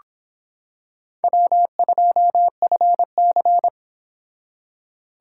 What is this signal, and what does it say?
Morse code, heard as a keyed tone, about 26 wpm, 700 Hz. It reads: W2FC